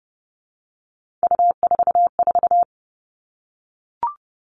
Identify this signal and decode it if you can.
Morse code, heard as a keyed tone, about 30 wpm, 700 Hz. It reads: U44
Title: All Cut Numbers - 3-Digit 30wpm